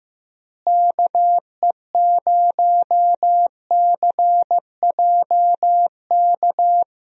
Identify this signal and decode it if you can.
Morse code, heard as a keyed tone, about 15 wpm, 700 Hz. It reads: KE0CJK
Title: USA Call Signs - 6 Characters - Repeated 2x - 15wpm